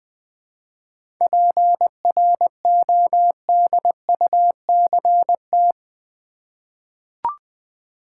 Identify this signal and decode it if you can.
Morse code, heard as a keyed tone, about 20 wpm, 700 Hz. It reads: PRODUCT
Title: 7 Letter Words - First 3 Letters Given - 20wpm